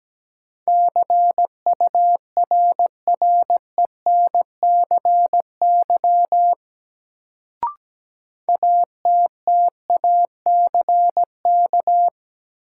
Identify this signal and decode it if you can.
Morse code, heard as a keyed tone, about 17 wpm, 700 Hz. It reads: CURRENCY ATTACK
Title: Sentences from Top 1500 Words - New Word Drill - 17wpm